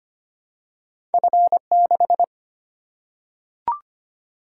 Morse code, heard as a keyed tone, about 25 wpm, 700 Hz. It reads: F6
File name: Two Letter-Number Combinations - 25wpm